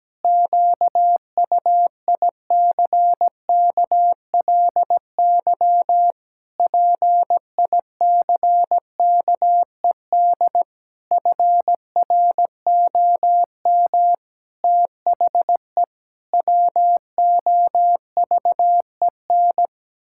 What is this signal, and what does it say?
Morse code, heard as a keyed tone, about 17 wpm, 700 Hz. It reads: QUICKLY PICKED FROM THE WOVEN
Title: Pangram Challenge v2 - 17wpm